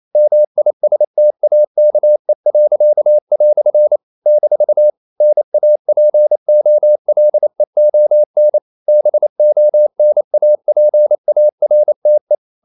Morse code, heard as a keyed tone, about 28 wpm, 600 Hz. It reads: MISTAKE." - NAPOLEON BONAPARTE
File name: CW / Practice / EnglishQuotations0128